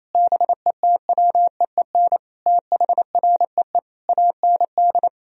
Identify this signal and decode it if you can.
Morse code, heard as a keyed tone, about 28 wpm, 700 Hz. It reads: BETWEEN THREE AND
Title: Sentences from Top 1000 Words - Review Part 3 - 28wpm